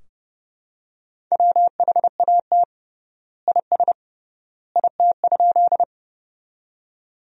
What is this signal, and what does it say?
Morse code, heard as a keyed tone, about 32 wpm, 700 Hz. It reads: WHAT IS IT?
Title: 3x Word-Spacing - Sentences from Top 100 Words - 30wpm